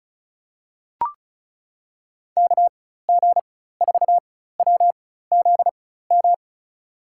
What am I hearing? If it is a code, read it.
Morse code, heard as a keyed tone, about 34 wpm, spaced as for 16 wpm, 700 Hz. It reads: KG4WZM
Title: USA Call Signs - 6 Characters - 4x Character Spacing - 35wpm